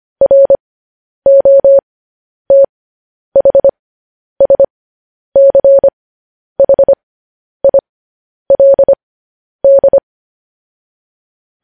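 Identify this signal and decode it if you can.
Morse code, heard as a keyed tone, about 25 wpm, spaced as for 10 wpm, 550 Hz. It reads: ROTHSCHILD
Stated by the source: Morse code news headlines broadcast, opening with vvv.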